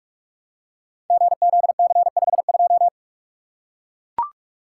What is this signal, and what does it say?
Morse code, heard as a keyed tone, about 42 wpm, 700 Hz. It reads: GZC52
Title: Code Groups - 5 Letter:Numbers - 45wpm